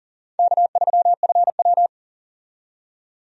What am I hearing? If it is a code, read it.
Morse code, heard as a keyed tone, about 40 wpm, 700 Hz. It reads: K3FW
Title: USA Call Signs - 4 Characters - 40wpm